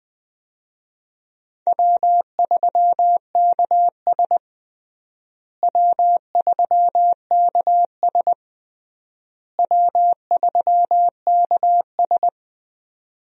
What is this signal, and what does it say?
Morse code, heard as a keyed tone, about 20 wpm, 700 Hz. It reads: W3KS W3KS W3KS
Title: USA Call Signs - 4 Characters - Repeated 3x - 20wpm